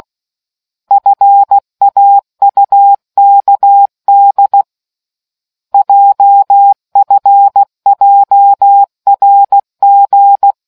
Morse code, heard as a keyed tone, about 16 wpm, 800 Hz. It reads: FAUKD JFJRG